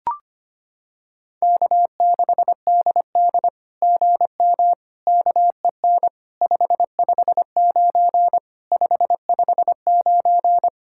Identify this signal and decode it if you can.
Morse code, heard as a keyed tone, about 25 wpm, 700 Hz. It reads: K6DD GM KEN 559 559
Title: SOTA QSO Practice 25wpm